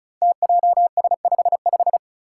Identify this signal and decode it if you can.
Morse code, heard as a keyed tone, about 36 wpm, 700 Hz. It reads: TJS55
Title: Five Challenging Letter-Number Combinations 35wpm